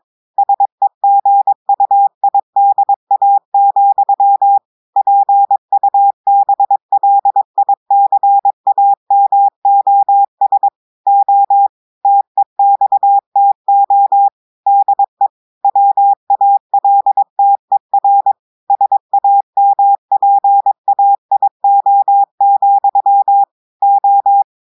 Morse code, heard as a keyed tone, about 22 wpm, 800 Hz. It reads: SEGUIDA, PUBLICAMOS O TEXTO DE WALTER SAMPAIO, O
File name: audios / portugais / portugais texte en clair 1 vitesse 22 mots minutes